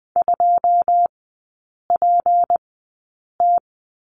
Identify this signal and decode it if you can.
Morse code, heard as a keyed tone, about 20 wpm, spaced as for 8 wpm, 700 Hz. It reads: 2PT
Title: 205 Lesson 33e - Letter j - Three Characters 20wpm